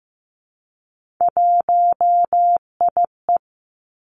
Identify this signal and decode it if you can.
Morse code, heard as a keyed tone, about 15 wpm, 700 Hz. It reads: 1IE
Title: Three Challenging Letter-Number Combinations 15wpm